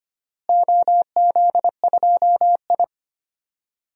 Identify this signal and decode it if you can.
Morse code, heard as a keyed tone, about 25 wpm, 700 Hz. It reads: OZ2I
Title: DXCC-100 Call Signs - 4 Characters - Repeated 2x - 25wpm